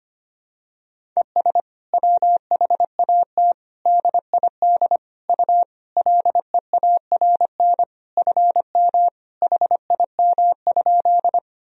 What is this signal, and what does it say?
Morse code, heard as a keyed tone, about 25 wpm, 700 Hz. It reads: ES WHAT DID U LEARN FM HIM?
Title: Sentences from Top 300 Words and Abbreviations 25wpm